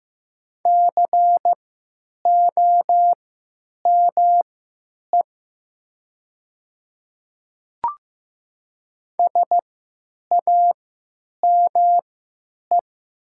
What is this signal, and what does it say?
Morse code, heard as a keyed tone, about 15 wpm, spaced as for 9 wpm, 700 Hz. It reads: COME SAME
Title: 3x Character Spacing - Top 200 Words - 15wpm